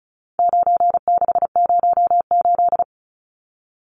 Morse code, heard as a keyed tone, about 34 wpm, 700 Hz. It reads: 9608
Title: Numbers - 4-Digit 35wpm